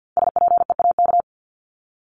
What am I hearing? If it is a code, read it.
Morse code, heard as a keyed tone, about 50 wpm, 700 Hz. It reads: SPEAK
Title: Top 1001-1250 Words 50wpm